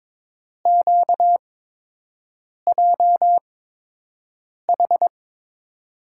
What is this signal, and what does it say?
Morse code, heard as a keyed tone, about 22 wpm, spaced as for 6 wpm, 700 Hz. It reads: QJH
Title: USA Call Signs - 5 Characters - 8x Character Spacing - 22wpm